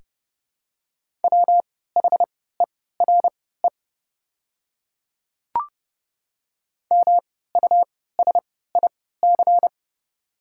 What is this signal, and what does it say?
Morse code, heard as a keyed tone, about 30 wpm, spaced as for 17 wpm, 700 Hz. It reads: WHERE MUSIC
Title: 3x Character Spacing - Top 500 Words - 30wpm